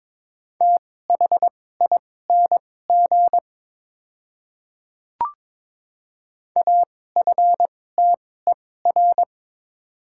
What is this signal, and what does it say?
Morse code, heard as a keyed tone, about 22 wpm, spaced as for 16 wpm, 700 Hz. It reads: THING AFTER
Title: 2x Character Spacing - Top 200 Words - 22wpm